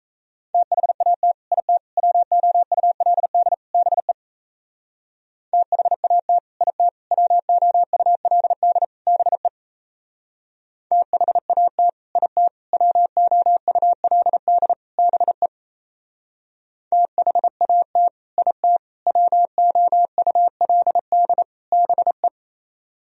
Morse code, heard as a keyed tone, about 36 wpm, 700 Hz. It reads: THAT IT WOULD BE THAT IT WOULD BE THAT IT WOULD BE THAT IT WOULD BE